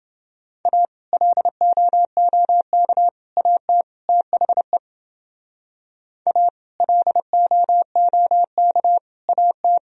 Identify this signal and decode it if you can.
Morse code, heard as a keyed tone, about 30 wpm, 700 Hz. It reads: A LOOK AT THE A LOOK AT